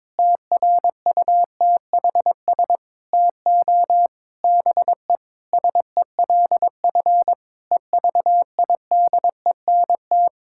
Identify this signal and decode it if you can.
Morse code, heard as a keyed tone, about 21 wpm, 700 Hz. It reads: TRUTHS TO BE SELF EVIDENT
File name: Sentences with 9-Word Phrases - 22wpm